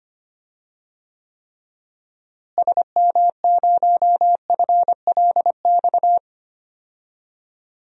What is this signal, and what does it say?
Morse code, heard as a keyed tone, about 25 wpm, 700 Hz. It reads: SM0FLX
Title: DXCC-100 Call Signs - 6 Characters - Repeated 3x - 25wpm